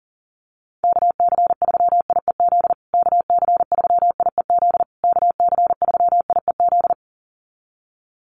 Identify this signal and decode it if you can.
Morse code, heard as a keyed tone, about 40 wpm, 700 Hz. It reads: KC3IEZ KC3IEZ KC3IEZ
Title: USA Call Signs Repeated 3x 40wpm